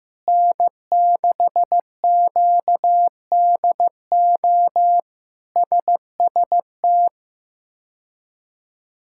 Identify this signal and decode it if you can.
Morse code, heard as a keyed tone, about 15 wpm, 700 Hz. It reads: N6QDO SST
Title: K1USN SST QSO Practice - 15wpm